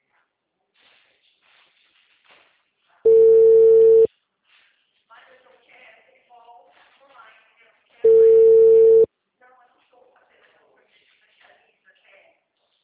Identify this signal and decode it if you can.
Ringback tone (China, 450 Hz), with speech.